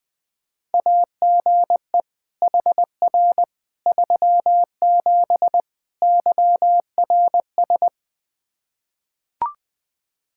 Morse code, heard as a keyed tone, about 21 wpm, 700 Hz. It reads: AGE HR 37 YRS
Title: Practice QSO - Exchange 3 - More Info - 20wpm